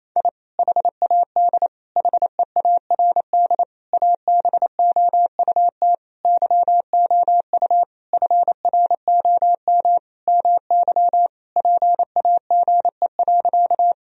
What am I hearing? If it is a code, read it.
Morse code, heard as a keyed tone, about 28 wpm, 700 Hz. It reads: I HAD HEARD ABOUT YOU FROM MY PAGE.
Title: Sentences from Top 1000 Words - Review Part 1 - 28wpm